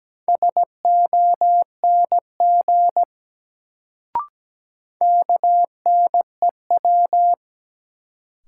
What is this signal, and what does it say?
Morse code, heard as a keyed tone, about 17 wpm, 700 Hz. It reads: SONG KNEW
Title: Sentences from Top 400 Words - New Word Drill - 17wpm